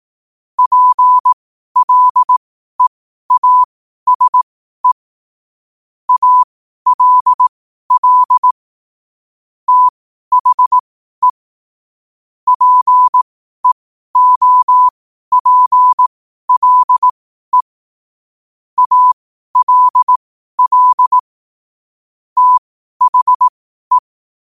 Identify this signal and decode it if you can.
Morse code, heard as a keyed tone, about 18 wpm, spaced as for 12 wpm, 1000 Hz. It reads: PLEASE ALL THE PEOPLE ALL THE
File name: qotdfiles / 12wpm